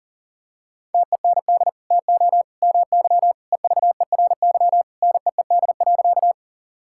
Morse code, heard as a keyed tone, about 40 wpm, 700 Hz. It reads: TEND TO MY EVERY NEED.
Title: Sentences from Top 1000 Words - Review Part 5 - 40wpm